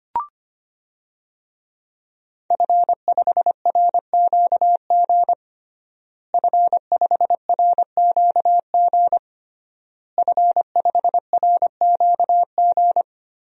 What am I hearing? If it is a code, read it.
Morse code, heard as a keyed tone, about 25 wpm, 700 Hz. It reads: F5RQG F5RQG F5RQG
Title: DXCC-100 Call Signs - 5 Characters - Repeated 3x - 25wpm